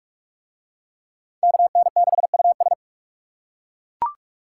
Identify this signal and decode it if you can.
Morse code, heard as a keyed tone, about 40 wpm, 700 Hz. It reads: KN6US